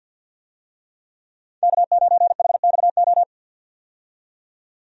Morse code, heard as a keyed tone, about 48 wpm, 700 Hz. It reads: K9HXO
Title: USA Call Signs - 5 Characters - 50wpm